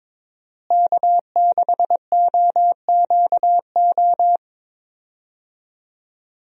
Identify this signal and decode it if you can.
Morse code, heard as a keyed tone, about 22 wpm, 700 Hz. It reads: K6OQO
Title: USA Call Signs - 5 Characters - Repeated 3x - 22wpm